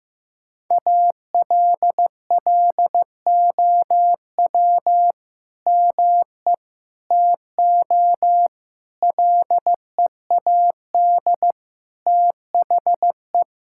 Morse code, heard as a keyed tone, about 15 wpm, 700 Hz. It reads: ALLOW ME TO LEAD THE
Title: Sentences from Top 1000 Words - Review Part 5 - 15wpm